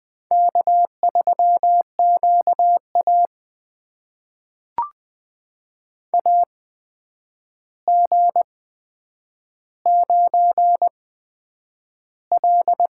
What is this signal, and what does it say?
Morse code, heard as a keyed tone, about 20 wpm, spaced as for 9 wpm, 700 Hz. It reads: K3QA AG9L
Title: USA Call Signs - 8x Character Spacing - 20wpm